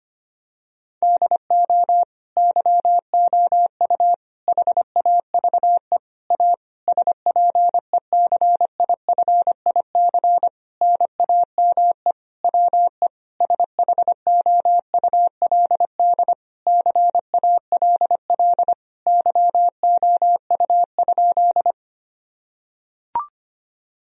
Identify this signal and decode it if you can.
Morse code, heard as a keyed tone, about 25 wpm, 700 Hz. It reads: DO YOU HAVE A SPECIFIC NAME WE SHOULD CALL YOU?
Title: Sentences from Top 1000 Words - Review Part 3 - 25wpm